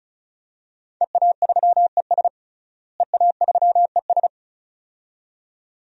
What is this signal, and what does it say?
Morse code, heard as a keyed tone, about 36 wpm, 700 Hz. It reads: EA3ES EA3ES
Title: DXCC-100 Call Signs - 5 Characters - Repeated 2x - 35wpm